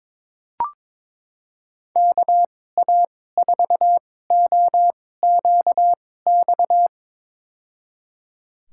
Morse code, heard as a keyed tone, about 22 wpm, spaced as for 16 wpm, 700 Hz. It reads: KA4OQX
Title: USA Call Signs - 6 Characters - 2x Character Spacing - 22wpm